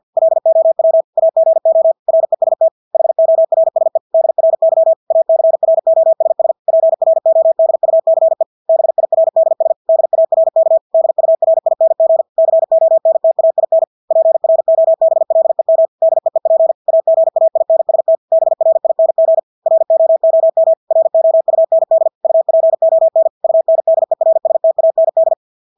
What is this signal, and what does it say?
Morse code, heard as a keyed tone, about 50 wpm, 650 Hz. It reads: LOW AGO REST HORSE DRY ACROSS PRODUCE BIRDS DARK DURING CONTAIN PROBLEM DEEP AGAINST BRING ROOM ROUND UPON UNDERSTAND
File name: Engelske ord 50 wpm